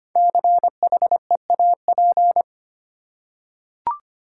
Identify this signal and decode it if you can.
Morse code, heard as a keyed tone, about 25 wpm, 700 Hz. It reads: CHEAP
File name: 5 Letter Words - First 3 Letters Given - 25wpm